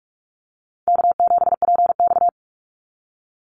Morse code, heard as a keyed tone, about 46 wpm, 700 Hz. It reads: K7PX
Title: NA SOTA Activator Callsigns 45wpm